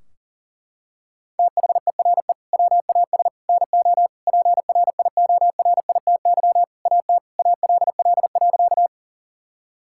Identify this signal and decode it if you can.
Morse code, heard as a keyed tone, about 37 wpm, 700 Hz. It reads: THERE WAS NO PRIORITY AT ALL.
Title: Sentences from Top 1600 Words 40wpm